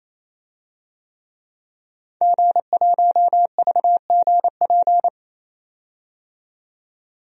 Morse code, heard as a keyed tone, about 28 wpm, 700 Hz. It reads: G1VGP
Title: DXCC-100 Call Signs - 5 Characters - 28wpm